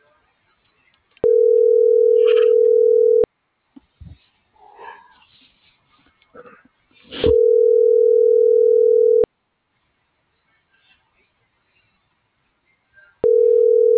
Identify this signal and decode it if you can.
Ringback tone (North America), 440 + 480 Hz.